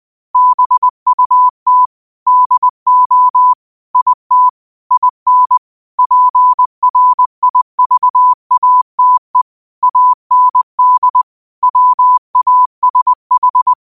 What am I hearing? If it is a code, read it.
Morse code, heard as a keyed tone, about 20 wpm, 1000 Hz. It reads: BUT DO IT IN PRIVATE AND WASH
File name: qotdfiles / 20wpm